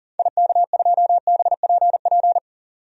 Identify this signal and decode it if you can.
Morse code, heard as a keyed tone, about 39 wpm, 700 Hz. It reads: IK2BPP
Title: DXCC-100 Call Signs - 6 Characters - 40wpm